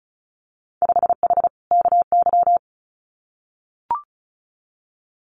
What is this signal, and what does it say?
Morse code, heard as a keyed tone, about 36 wpm, 700 Hz. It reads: 5H KY
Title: Winter Field Day Exchanges Encore 35wpm